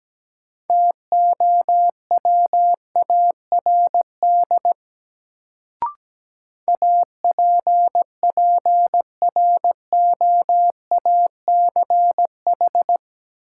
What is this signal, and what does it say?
Morse code, heard as a keyed tone, about 17 wpm, 700 Hz. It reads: TOWARD APPROACH
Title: Sentences from Top 1000 Words - Review Part 6 - New Word Drill - 17wpm